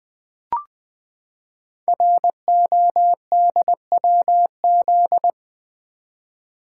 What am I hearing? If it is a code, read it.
Morse code, heard as a keyed tone, about 20 wpm, 700 Hz. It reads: RODWZ
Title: Code Groups - 5 Letters - 20wpm